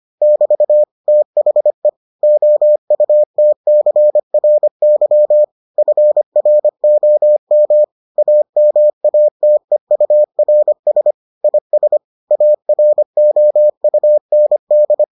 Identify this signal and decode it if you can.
Morse code, heard as a keyed tone, about 25 wpm, 600 Hz. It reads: <BT> THE OUTCRY FROM AMATEURS IS AROUND